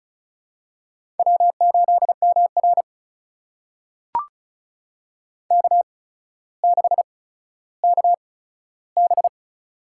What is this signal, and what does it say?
Morse code, heard as a keyed tone, about 36 wpm, spaced as for 15 wpm, 700 Hz. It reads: W8MR K6KB
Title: USA Call Signs - 4 Characters - 8x Character Spacing - 35wpm